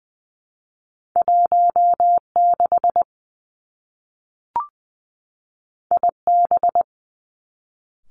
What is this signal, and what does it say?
Morse code, heard as a keyed tone, about 20 wpm, 700 Hz. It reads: 1 6 I B
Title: Two Challenging Letter-Number Combinations 20wpm